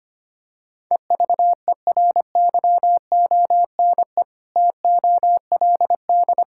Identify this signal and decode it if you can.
Morse code, heard as a keyed tone, about 25 wpm, 700 Hz. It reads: EVERYONE TOLD
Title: Sentences from Top 1000 Words - Review Part 5 - 25wpm